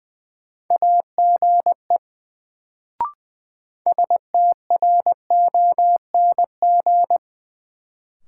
Morse code, heard as a keyed tone, about 20 wpm, 700 Hz. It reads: AGE STRONG
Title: Sentences from Top 500 Words - New Word Drill - 20wpm